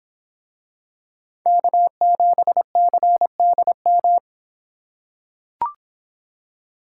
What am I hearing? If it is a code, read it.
Morse code, heard as a keyed tone, about 24 wpm, 700 Hz. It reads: K7CDM